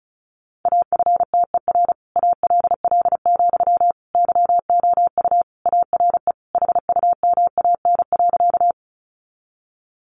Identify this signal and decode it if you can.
Morse code, heard as a keyed tone, about 34 wpm, 700 Hz. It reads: AFTER ALL, YOU ARE HUMAN.
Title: Sentences from Top 600 Words 35wpm